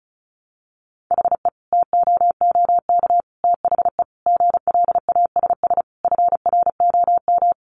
Morse code, heard as a keyed tone, about 34 wpm, 700 Hz. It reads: HE TOOK THE GLASS FROM
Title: Sentences from Top 1300 Words 35wpm